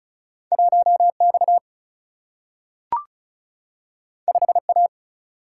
Morse code, heard as a keyed tone, about 36 wpm, 700 Hz. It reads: 1 X 5 A
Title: Single-Letter Single-Number 35wpm